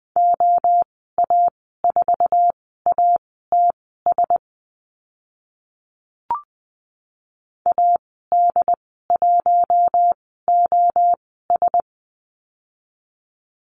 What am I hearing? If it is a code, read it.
Morse code, heard as a keyed tone, about 20 wpm, spaced as for 15 wpm, 700 Hz. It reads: OA4ATS AD1OS
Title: 069 Lesson 12g - Letter d - Call Signs 20wpm